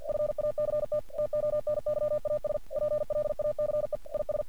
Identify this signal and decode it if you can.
Morse code, heard as a keyed tone, about 49 wpm, 600 Hz. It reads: FACT TONYAS PLACE IS
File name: archived / CRSnet